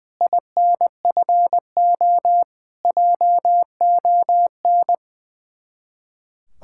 Morse code, heard as a keyed tone, about 19 wpm, 700 Hz. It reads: INFO JON
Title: Speed-Racing Common QSO Phrases - 20wpm